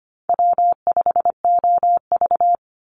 Morse code, heard as a keyed tone, about 25 wpm, 700 Hz. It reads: W5OV
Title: Easy Call Signs - 25wpm